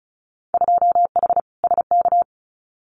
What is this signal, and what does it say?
Morse code, heard as a keyed tone, about 34 wpm, 700 Hz. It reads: 2H SK
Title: Winter Field Day Exchanges Encore 35wpm